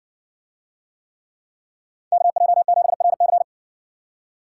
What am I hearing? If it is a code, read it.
Morse code, heard as a keyed tone, about 60 wpm, 700 Hz. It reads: KJ7RC